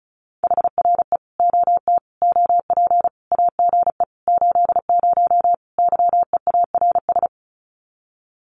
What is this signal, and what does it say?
Morse code, heard as a keyed tone, about 34 wpm, 700 Hz. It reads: HRE OT OP AGE 80 YEARS
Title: Common QSO Phrases 35wpm